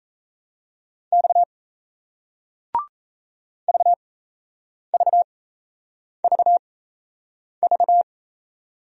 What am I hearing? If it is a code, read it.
Morse code, heard as a keyed tone, about 35 wpm, spaced as for 8 wpm, 700 Hz. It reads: X VVVV